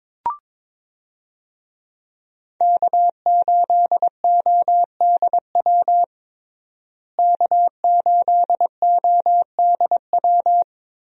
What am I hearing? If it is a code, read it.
Morse code, heard as a keyed tone, about 22 wpm, 700 Hz. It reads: K8ODW K8ODW
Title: USA Call Signs - 5 Characters - Repeated 3x - 22wpm